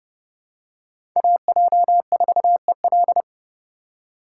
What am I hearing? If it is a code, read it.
Morse code, heard as a keyed tone, about 32 wpm, 700 Hz. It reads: AJ4EL
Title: USA Call Signs - 5 Characters - 4x Character Spacing - 30wpm